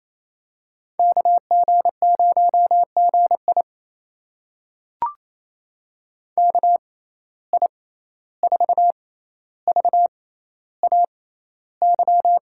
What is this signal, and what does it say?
Morse code, heard as a keyed tone, about 28 wpm, spaced as for 14 wpm, 700 Hz. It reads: KG0GI KI4VAY
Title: USA Call Signs - 6x Character Spacing - 28wpm